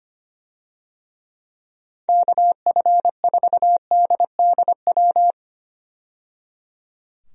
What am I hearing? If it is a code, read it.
Morse code, heard as a keyed tone, about 25 wpm, 700 Hz. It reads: KF4DDW
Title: USA Call Signs - 6 Characters - 25wpm